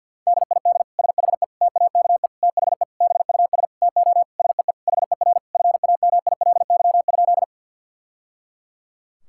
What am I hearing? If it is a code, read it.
Morse code, heard as a keyed tone, about 50 wpm, 700 Hz. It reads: DID SHE TAKE THE BUS TO SEE HER FAMILY?
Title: Sentences from Top 1000 Words - Review Part 6 - 50wpm